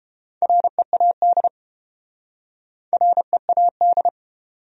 Morse code, heard as a keyed tone, about 31 wpm, 700 Hz. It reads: READ READ